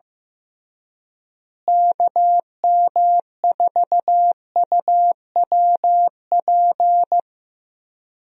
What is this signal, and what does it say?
Morse code, heard as a keyed tone, about 15 wpm, 700 Hz. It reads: KM4UWP
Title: USA Call Signs - 6 Characters - Repeated 3x - 15wpm